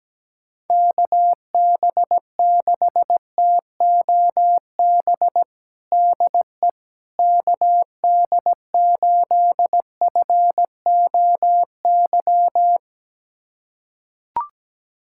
Morse code, heard as a keyed tone, about 17 wpm, 700 Hz. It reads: KB6TOB DE KD8FOY
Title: USA Callsign DE Callsign 17wpm